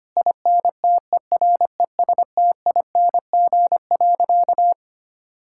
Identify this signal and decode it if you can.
Morse code, heard as a keyed tone, about 25 wpm, 700 Hz. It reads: INTERESTING.
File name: practice / 25wpm / Sentences with 3-Word Phrases - 25wpm